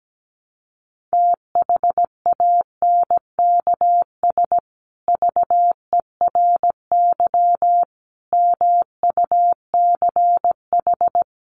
Morse code, heard as a keyed tone, about 17 wpm, 700 Hz. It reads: THANKS VERY MUCH
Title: Sets of 5 Words Encore2 17wpm